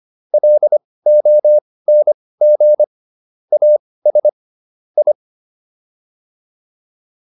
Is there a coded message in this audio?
Yes: Morse code from 0.3 to 5.1 s, heard as a keyed tone, about 25 wpm, spaced as for 18 wpm, 600 Hz. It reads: LONG AS I